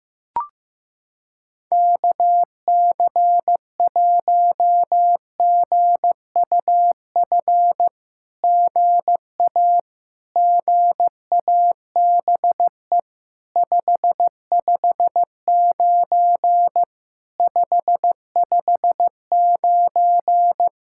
Morse code, heard as a keyed tone, about 15 wpm, 700 Hz. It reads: KC1GUF GA GABE 559 559
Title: SOTA QSO Practice 15wpm